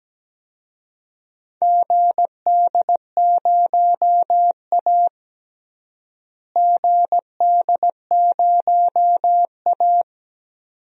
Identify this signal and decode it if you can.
Morse code, heard as a keyed tone, about 17 wpm, 700 Hz. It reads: GD0A GD0A
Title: DXCC-100 Call Signs - 4 Characters - Repeated 2x - 17wpm